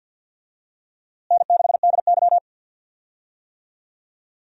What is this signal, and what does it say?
Morse code, heard as a keyed tone, about 46 wpm, 700 Hz. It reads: N6DY
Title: USA Call Signs - 4 Characters - Repeated 3x - 50wpm